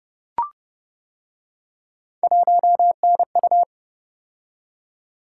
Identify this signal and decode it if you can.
Morse code, heard as a keyed tone, about 29 wpm, 700 Hz. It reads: 1NU